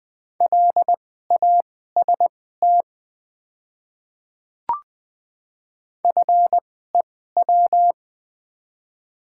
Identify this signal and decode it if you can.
Morse code, heard as a keyed tone, about 20 wpm, spaced as for 14 wpm, 700 Hz. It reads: LAST FEW
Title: 2x Character Spacing - Top 300 Words - 20wpm